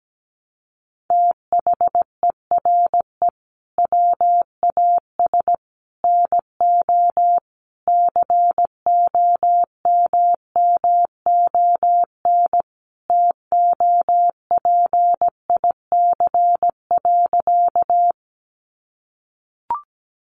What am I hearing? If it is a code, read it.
Morse code, heard as a keyed tone, about 17 wpm, 700 Hz. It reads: THERE WAS NO COMMON TOPIC.
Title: Sentences from Top 1200 Words 17wpm